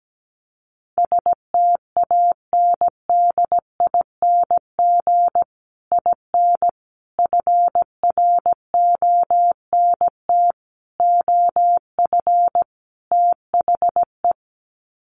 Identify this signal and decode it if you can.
Morse code, heard as a keyed tone, about 17 wpm, 700 Hz. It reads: STANDING IN FRONT OF THE
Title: Sets of 5 Words Encore2 17wpm